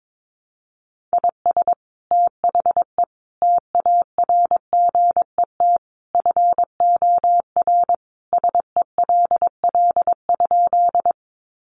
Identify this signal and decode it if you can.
Morse code, heard as a keyed tone, about 22 wpm, 700 Hz. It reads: IS THE TARGET FOR SELL?
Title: Sentences from Top 1200 Words 22wpm